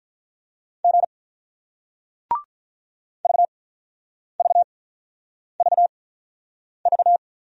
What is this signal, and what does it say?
Morse code, heard as a keyed tone, about 50 wpm, spaced as for 9 wpm, 700 Hz. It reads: G VVVV